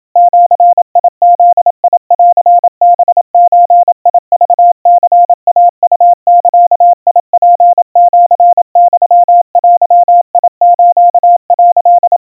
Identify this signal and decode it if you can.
Morse code (Japanese Wabun code), heard as a keyed tone, about 27 wpm, 700 Hz. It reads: ジブンハゾクニイウザツシユデス」
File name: kawa / Morse sample / kamyu 100 cpm 700 Hz Crispy